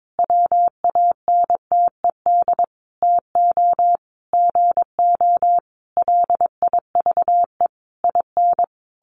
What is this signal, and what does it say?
Morse code, heard as a keyed tone, about 22 wpm, 700 Hz. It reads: WANTED TO GO LIVE IN
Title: A Sound Of Thunder 22wpm